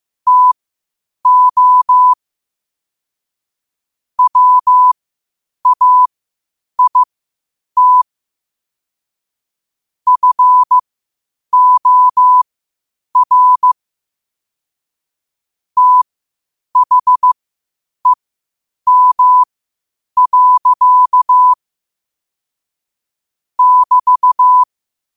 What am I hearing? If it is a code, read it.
Morse code, heard as a keyed tone, about 15 wpm, spaced as for 8 wpm, 1000 Hz. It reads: TO WAIT FOR THEM. <BT>
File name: qotdfiles / 8wpm